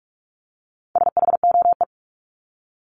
Morse code, heard as a keyed tone, about 44 wpm, 700 Hz. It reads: SHOE
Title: Top 1751-2000 Words 45wpm